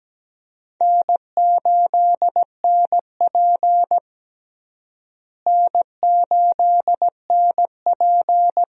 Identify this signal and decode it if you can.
Morse code, heard as a keyed tone, about 17 wpm, 700 Hz. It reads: N8NP N8NP
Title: USA Call Signs - 4 Characters - Repeated 2x - 17wpm